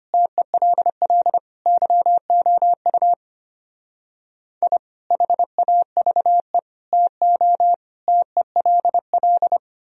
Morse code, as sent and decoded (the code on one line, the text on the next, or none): - . .-.. .-.. / -.-- --- ..- / .. / .... .- ...- . / - --- / - . .-.. .-..
TELL YOU I HAVE TO TELL